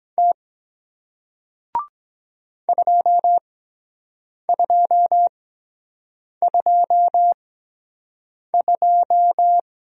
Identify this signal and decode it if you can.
Morse code, heard as a keyed tone, about 23 wpm, spaced as for 7 wpm, 700 Hz. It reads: T 2222